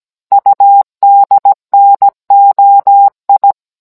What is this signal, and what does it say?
Morse code, heard as a keyed tone, about 17 wpm, 800 Hz. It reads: UDNOI